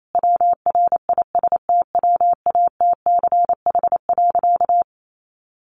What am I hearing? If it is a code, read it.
Morse code, heard as a keyed tone, about 28 wpm, 700 Hz. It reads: WRISTWATCH.
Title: A Sound Of Thunder 28wpm